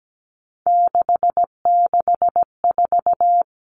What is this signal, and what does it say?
Morse code, heard as a keyed tone, about 17 wpm, 700 Hz. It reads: 664
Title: Common Cut Numbers - 3-Digit 17wpm